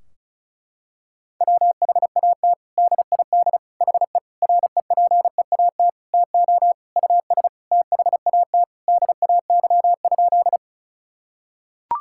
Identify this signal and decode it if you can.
Morse code, heard as a keyed tone, about 34 wpm, 700 Hz. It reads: WHAT DID HE REPEAT TO US THAT DAY?
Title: Sentences from Top 1600 Words 35wpm